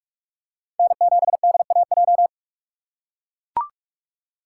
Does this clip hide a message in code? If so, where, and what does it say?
Yes: Morse code from 0.8 to 2.3 s, heard as a keyed tone, about 42 wpm, 700 Hz. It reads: N7DAJ